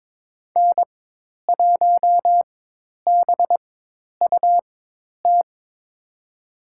Morse code, heard as a keyed tone, about 22 wpm, spaced as for 10 wpm, 700 Hz. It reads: N1BUT
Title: USA Call Signs - 5 Characters - 4x Character Spacing - 22wpm